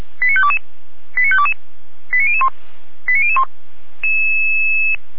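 Five-tone selective call (ZVEI), heard as quick runs of five tones, tones of about 70 ms. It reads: 89522 89522 89001 89001
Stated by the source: ZVEI five-tone selective call received by SDR.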